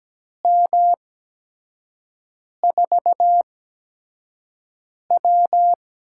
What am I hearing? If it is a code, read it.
Morse code, heard as a keyed tone, about 17 wpm, spaced as for 5 wpm, 700 Hz. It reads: M4W
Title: USA Call Signs - 6 Characters - 8x Character Spacing - 17wpm